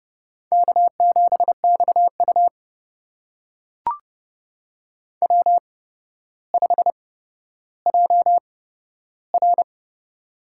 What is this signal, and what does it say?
Morse code, heard as a keyed tone, about 30 wpm, spaced as for 13 wpm, 700 Hz. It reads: K7XU W5JR
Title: USA Call Signs - 4 Characters - 8x Character Spacing - 30wpm